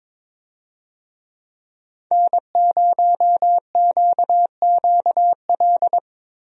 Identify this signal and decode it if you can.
Morse code, heard as a keyed tone, about 22 wpm, 700 Hz. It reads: N0QQL
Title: USA Call Signs - 5 Characters - Repeated 3x - 22wpm